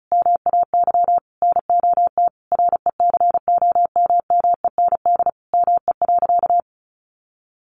Morse code, heard as a keyed tone, about 34 wpm, 700 Hz. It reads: MAY NOT RECOMMEND ME.
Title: Sentences from Top 1200 Words 35wpm